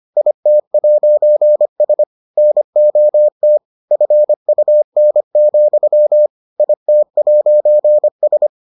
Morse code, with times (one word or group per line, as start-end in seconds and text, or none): 0.2-2.0 IT'S
2.4-3.6 NOT
3.9-6.3 FUN,
6.6-8.5 IT'S